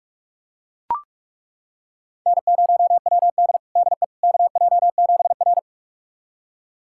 Morse code, heard as a keyed tone, about 41 wpm, 700 Hz. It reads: N0WD DE KJ7R
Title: USA Callsign DE Callsign - 4 Characters - 45wpm